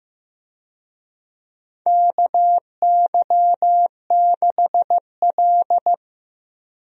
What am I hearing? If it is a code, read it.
Morse code, heard as a keyed tone, about 15 wpm, 700 Hz. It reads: KY6L
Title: USA Call Signs - 4 Characters - Repeated 2x - 15wpm